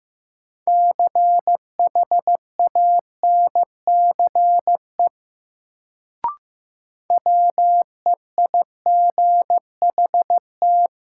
Morse code, heard as a keyed tone, about 15 wpm, 700 Hz. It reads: CHANCE WEIGHT
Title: Sentences from Top 800 Words - New Word Drill - 15wpm